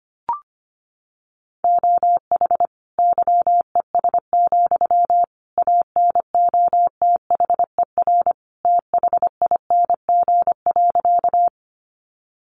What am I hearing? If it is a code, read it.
Morse code, heard as a keyed tone, about 25 wpm, 700 Hz. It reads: OH YES, ANOTHER THING.
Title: Sentences from Top 600 Words 25wpm